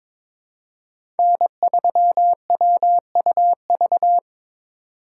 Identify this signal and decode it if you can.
Morse code, heard as a keyed tone, about 22 wpm, 700 Hz. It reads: N3WUV
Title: USA Call Signs - 8x Character Spacing - 22wpm